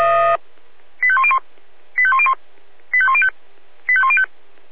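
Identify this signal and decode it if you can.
Five-tone selective call (ZVEI), heard as quick runs of five tones, tones of about 70 ms. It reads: 85101 85101 85105 85105
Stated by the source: ZVEI five-tone selective call 85101 received on SDR.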